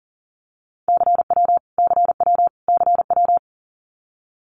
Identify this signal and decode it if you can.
Morse code, heard as a keyed tone, about 40 wpm, 700 Hz. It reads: CW CW CW
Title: Abbreviations Repeated 3x - 40wpm